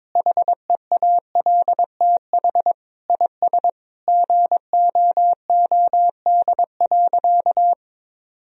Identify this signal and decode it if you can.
Morse code, heard as a keyed tone, about 22 wpm, 700 Hz. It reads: HEALTH IS GOOD.
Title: Sentences from Top 1000 Words - Review Part 3 - 22wpm